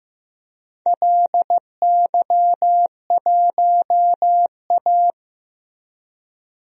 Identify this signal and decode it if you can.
Morse code, heard as a keyed tone, about 16 wpm, 700 Hz. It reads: LY1A
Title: DXCC-100 Call Signs - 4 Characters - Repeated 2x - 15wpm